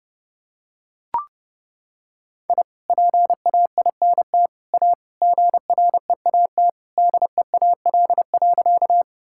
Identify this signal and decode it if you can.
Morse code, heard as a keyed tone, about 32 wpm, 700 Hz. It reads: I PAINT A GREAT DEAL.
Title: Sentences from Top 1000 Words - Review Part 6 - 30wpm